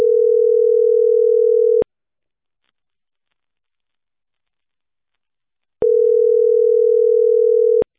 Ringback tone (North America), 440 + 480 Hz.